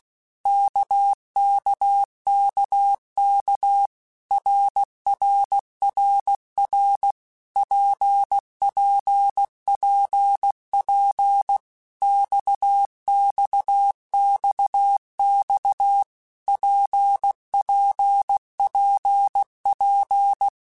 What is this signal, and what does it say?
Morse code, heard as a keyed tone, about 16 wpm, 800 Hz. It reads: KKKK RRRR PPPP XXXX PPPP
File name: Basic Letter Groups By7Groups / RKPX / RKPX@15